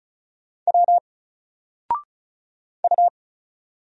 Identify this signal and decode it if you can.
Morse code, heard as a keyed tone, about 35 wpm, 700 Hz. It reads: W U
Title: Single Character (Letter:Number:Punctuation:Prosign) 35wpm